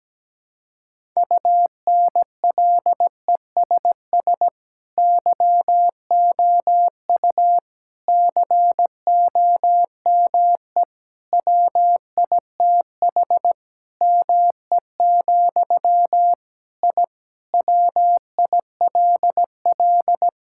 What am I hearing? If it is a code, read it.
Morse code, heard as a keyed tone, about 17 wpm, 700 Hz. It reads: UNLESS YOU COME WITH ME, I WILL
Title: Sentences from Top 1000 Words - Review Part 6 - 17wpm